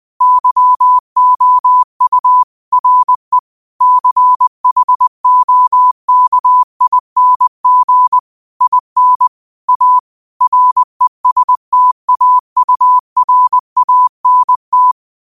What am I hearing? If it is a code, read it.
Morse code, heard as a keyed tone, about 20 wpm, 1000 Hz. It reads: YOU RE CHOKING IN A RESTAURANT
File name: 20wpm